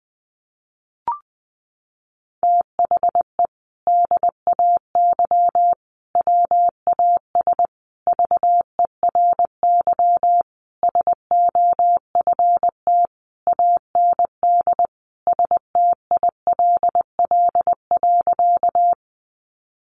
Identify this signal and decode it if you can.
Morse code, heard as a keyed tone, about 20 wpm, 700 Hz. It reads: THE DAY WAS VERY SOFT AND STILL.
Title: Sentences from Top 1500 Words 20wpm